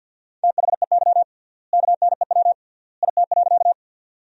Morse code, heard as a keyed tone, about 50 wpm, 700 Hz. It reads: THEY KNEW IT.
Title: 3x Word-Spacing - Sentences from Top 400 Words - 50wpm